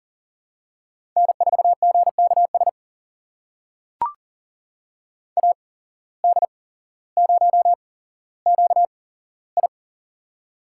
Morse code, heard as a keyed tone, about 40 wpm, spaced as for 17 wpm, 700 Hz. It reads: N4GKS AD0QI
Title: USA Call Signs - 5 Characters - 8x Character Spacing - 40wpm